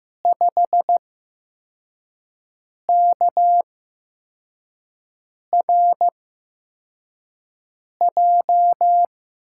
Morse code, heard as a keyed tone, about 15 wpm, spaced as for 4 wpm, 700 Hz. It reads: 5KRJ
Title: USA Call Signs - 6 Characters - 8x Character Spacing - 15wpm